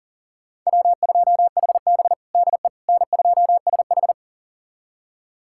Morse code, heard as a keyed tone, about 40 wpm, 700 Hz. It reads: W2HB DE N2SH
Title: USA Callsign DE Callsign - 4 Characters - 40wpm